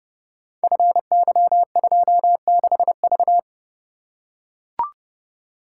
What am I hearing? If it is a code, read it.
Morse code, heard as a keyed tone, about 30 wpm, 700 Hz. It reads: FY26V
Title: Code Groups - 5 Letter:Numbers - 30wpm